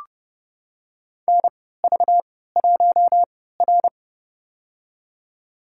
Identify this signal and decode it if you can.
Morse code, heard as a keyed tone, about 29 wpm, spaced as for 17 wpm, 700 Hz. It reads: NV1R
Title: USA Call Signs - 4 Characters - 3x Character Spacing - 30wpm